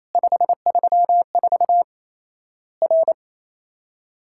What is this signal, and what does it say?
Morse code, heard as a keyed tone, about 28 wpm, 700 Hz. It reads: 534 R
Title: Worked All Europe DX Contest - QTC Exchange - 28wpm